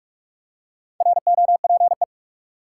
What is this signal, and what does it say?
Morse code, heard as a keyed tone, about 44 wpm, 700 Hz. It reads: ROPE
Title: Four Letter Words - 45wpm